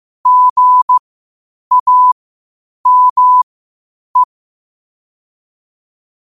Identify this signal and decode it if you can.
Morse code, heard as a keyed tone, about 15 wpm, spaced as for 8 wpm, 1000 Hz. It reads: GAME
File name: qotdfiles / 8wpm